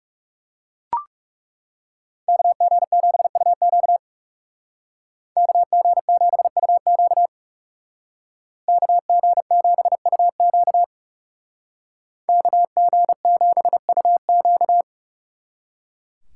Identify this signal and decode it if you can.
Morse code, heard as a keyed tone, about 35 wpm, 700 Hz. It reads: KG7UQ KG7UQ KG7UQ KG7UQ
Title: Speed Racing - USA 2x2 Call Signs 30wpm